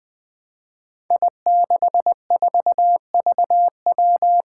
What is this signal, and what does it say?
Morse code, heard as a keyed tone, about 20 wpm, 700 Hz. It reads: I64VW
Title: Five Challenging Letter-Number Combinations 20wpm